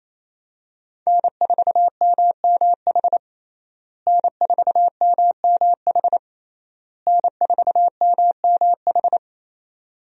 Morse code, heard as a keyed tone, about 27 wpm, 700 Hz. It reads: N4MMH N4MMH N4MMH
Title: USA Call Signs - 5 Characters - Repeated 3x - 28wpm